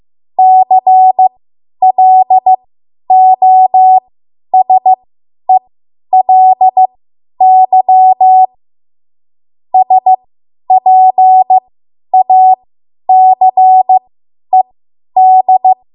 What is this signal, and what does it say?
Morse code, heard as a keyed tone, about 15 wpm, spaced as for 10 wpm, 750 Hz. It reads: CLOSELY SPACED
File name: Morse / 10WPM